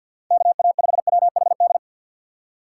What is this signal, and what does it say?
Morse code, heard as a keyed tone, about 52 wpm, 700 Hz. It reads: KA5WHD
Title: USA Call Signs - 6 Characters - 6x Character Spacing - 50wpm